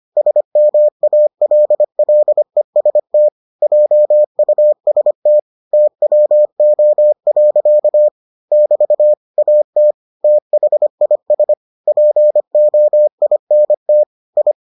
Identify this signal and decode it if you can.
Morse code, heard as a keyed tone, about 25 wpm, 600 Hz. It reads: SMALLEST JUST TWO. <BT> AT THIS POINT I